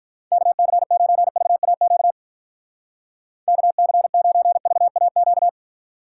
Morse code, heard as a keyed tone, about 42 wpm, 700 Hz. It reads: KC9VAQ KC9VAQ
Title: Speed Racing - USA 2x3 Call Signs 35wpm